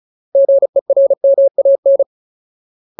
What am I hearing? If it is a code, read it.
Morse code, heard as a keyed tone, about 34 wpm, 550 Hz. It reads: GERMAN
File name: Ultimate Word List / 35WPM Ultimate